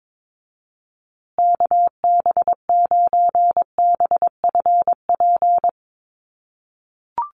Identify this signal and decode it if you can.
Morse code, heard as a keyed tone, about 22 wpm, 700 Hz. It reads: KB9BFP
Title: USA Call Signs Repeated 3x 22wpm